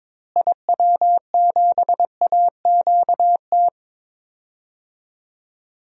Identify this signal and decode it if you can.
Morse code, heard as a keyed tone, about 22 wpm, 700 Hz. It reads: IW7AQT
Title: DXCC-100 Call Signs - 6 Characters - Repeated 2x - 22wpm